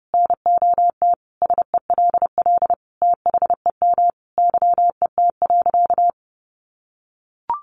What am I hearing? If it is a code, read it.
Morse code, heard as a keyed tone, about 28 wpm, 700 Hz. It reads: NOT SELL THEM YET.
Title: Sentences from Top 1100 Words 30wpm